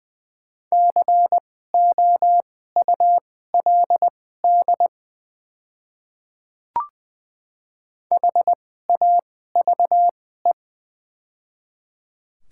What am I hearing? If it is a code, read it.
Morse code, heard as a keyed tone, about 20 wpm, spaced as for 14 wpm, 700 Hz. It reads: COULD HAVE
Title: 2x Character Spacing - Top 200 Words - 20wpm